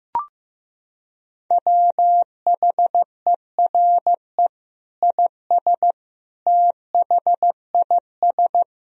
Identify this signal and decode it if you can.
Morse code, heard as a keyed tone, about 15 wpm, 700 Hz. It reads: WHERE IS THIS
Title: Sentences from Top 1000 Words - Review Part 3 - 15wpm